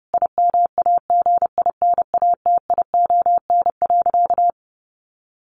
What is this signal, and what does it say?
Morse code, heard as a keyed tone, about 30 wpm, 700 Hz. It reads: IMAGINATION.
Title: Sentences from Top 1300 Words 30wpm